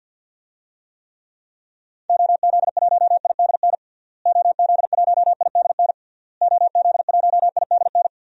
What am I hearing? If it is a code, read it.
Morse code, heard as a keyed tone, about 52 wpm, 700 Hz. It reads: OZ1IDN OZ1IDN OZ1IDN
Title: DXCC-100 Call Signs - 6 Characters - Repeated 3x - 50wpm